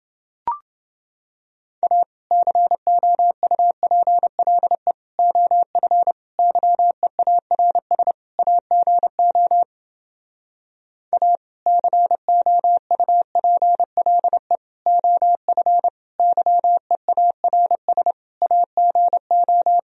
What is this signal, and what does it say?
Morse code, heard as a keyed tone, about 30 wpm, 700 Hz. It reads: A COUPLE OF YEARS AGO A COUPLE OF YEARS AGO